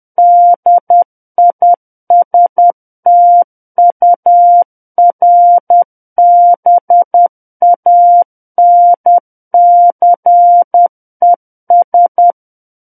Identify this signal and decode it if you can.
Morse code, heard as a keyed tone, about 10 wpm, 700 Hz. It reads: DISTURBANCES
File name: Book One / 01-Chapter 1 - 10 WPM